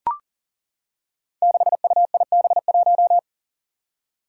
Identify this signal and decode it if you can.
Morse code, heard as a keyed tone, about 38 wpm, 700 Hz. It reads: 6UIB1
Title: Five Challenging Letter-Number Combinations 40wpm